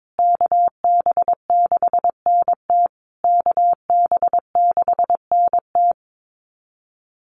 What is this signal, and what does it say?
Morse code, heard as a keyed tone, about 22 wpm, 700 Hz. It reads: KB6NT KB6NT
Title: USA Call Signs Repeated 3x 22wpm